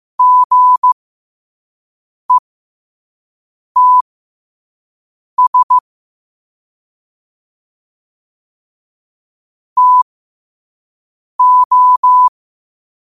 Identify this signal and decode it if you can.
Morse code, heard as a keyed tone, about 15 wpm, spaced as for 5 wpm, 1000 Hz. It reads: GETS TO
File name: qotdfiles / 5wpm